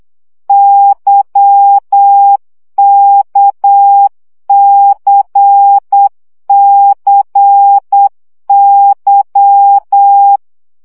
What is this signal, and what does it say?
Morse code, heard as a keyed tone, about 8 wpm, 800 Hz. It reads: YKCCY
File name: CW gyakorlas k c y 7 baud